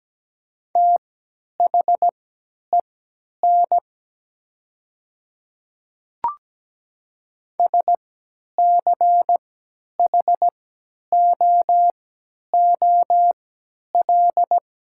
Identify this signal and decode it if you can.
Morse code, heard as a keyed tone, about 17 wpm, spaced as for 10 wpm, 700 Hz. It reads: THEN SCHOOL
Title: 3x Character Spacing - Top 300 Words - 17wpm